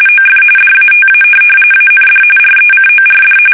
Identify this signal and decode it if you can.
RTTY, about 45.45 baud. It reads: OF DATA AND RECORD
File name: fsk850 / teletype-corp